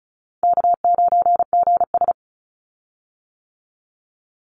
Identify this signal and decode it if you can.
Morse code, heard as a keyed tone, about 36 wpm, 700 Hz. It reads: K9GS
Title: Call Signs - 35wpm